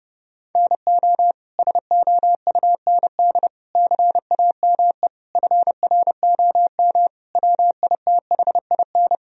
Morse code, heard as a keyed tone, about 30 wpm, 700 Hz. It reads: NO SOUND CAME FROM WITHIN
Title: Sentences from Top 1000 Words - Review Part 4 - 30wpm